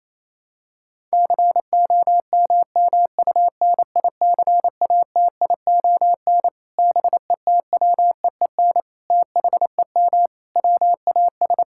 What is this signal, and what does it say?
Morse code, heard as a keyed tone, about 27 wpm, 700 Hz. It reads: COMMUNICATION BETWEEN THEM WAS
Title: Sentences from Top 1000 Words - Review Part 6 - 28wpm